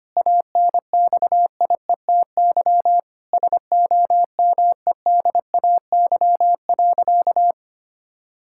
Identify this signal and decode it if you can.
Morse code, heard as a keyed tone, about 25 wpm, 700 Hz. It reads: ANXIETY SOMEDAY.
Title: Sentences from Top 1600 Words 25wpm